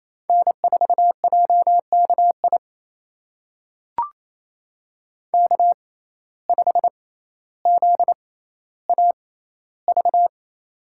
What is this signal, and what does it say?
Morse code, heard as a keyed tone, about 28 wpm, spaced as for 14 wpm, 700 Hz. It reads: N4JKI K5ZAV
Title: USA Call Signs - 5 Characters - 6x Character Spacing - 28wpm